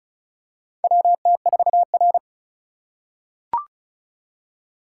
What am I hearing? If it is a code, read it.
Morse code, heard as a keyed tone, about 34 wpm, 700 Hz. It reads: WT4R
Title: USA Call Signs - 4 Characters - 4x Character Spacing - 35wpm